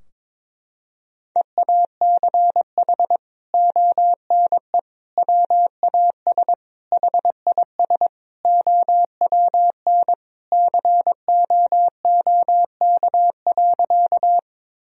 Morse code, heard as a keyed tone, about 21 wpm, 700 Hz. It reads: EACH ONE WAS HIS OWN COOK.
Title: Sentences from Top 1000 Words - Review Part 4 - 22wpm